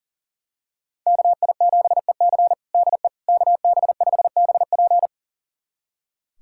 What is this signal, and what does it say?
Morse code, heard as a keyed tone, about 40 wpm, 700 Hz. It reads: KI7EC DE KB5BP
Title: USA Callsign DE Callsign - 5 Characters - 40wpm